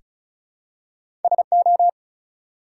Morse code, heard as a keyed tone, about 36 wpm, 700 Hz. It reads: SO
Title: Two Letter Words - 35wpm